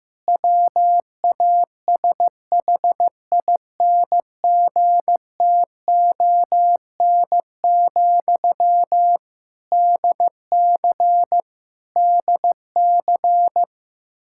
Morse code, heard as a keyed tone, about 15 wpm, 700 Hz. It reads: WASHINGTON, DC DC
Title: Practice QSO - Exchange 1 - Beginning - 15wpm